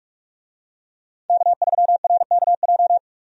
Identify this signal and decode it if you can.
Morse code, heard as a keyed tone, about 46 wpm, 700 Hz. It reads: K3RKJ
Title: USA Call Signs - 5 Characters - 45wpm